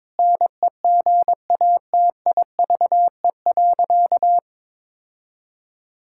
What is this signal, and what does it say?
Morse code, heard as a keyed tone, about 22 wpm, 700 Hz. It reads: NEGATIVE.
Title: Sentences from Top 1000 Words - Review Part 6 - 22wpm